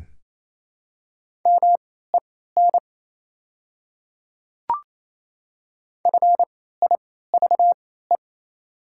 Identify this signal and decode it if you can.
Morse code, heard as a keyed tone, about 27 wpm, spaced as for 16 wpm, 690 Hz. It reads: MEN FIVE
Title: 3x Character Spacing - Top 500 Words - 28wpm